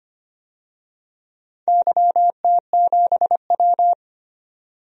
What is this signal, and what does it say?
Morse code, heard as a keyed tone, about 25 wpm, 700 Hz. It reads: YT7W
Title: DXCC-100 Call Signs - 4 Characters - 25wpm